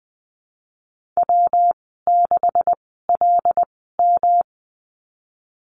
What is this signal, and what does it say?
Morse code, heard as a keyed tone, about 20 wpm, spaced as for 15 wpm, 700 Hz. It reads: W6LM
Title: 161 Lesson 26g - Letter g - Call Signs 20wpm